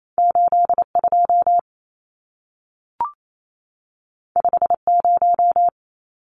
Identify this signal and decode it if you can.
Morse code, heard as a keyed tone, about 28 wpm, 700 Hz. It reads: 82 50
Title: Numbers - 2-Digit 28wpm